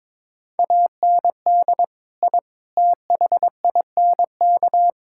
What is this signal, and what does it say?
Morse code, heard as a keyed tone, about 22 wpm, 700 Hz. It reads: AND I THINK
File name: practice / 22wpm / Sentences with 4-Word Phrases - 22wpm